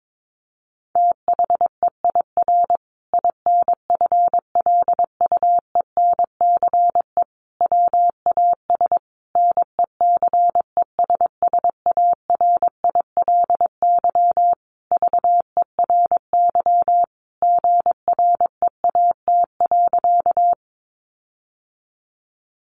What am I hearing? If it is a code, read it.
Morse code, heard as a keyed tone, about 22 wpm, 700 Hz. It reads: THEIR INFLUENCE WAS NECESSARILY VERY GREAT.
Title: Sentences from Top 1400 Words 22wpm